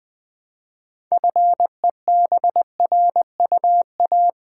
Morse code, heard as a keyed tone, about 20 wpm, 700 Hz. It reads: FEBRUA
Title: Build A Word - 2 Letters At A Time - 8-Letter Words - 20wpm